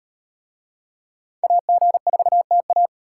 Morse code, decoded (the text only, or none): AG4TA